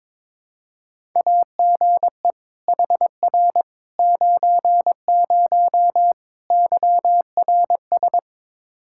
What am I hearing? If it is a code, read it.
Morse code, heard as a keyed tone, about 22 wpm, 700 Hz. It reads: AGE HR 90 YRS
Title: Practice QSO - Exchange 3 - More Info - 22wpm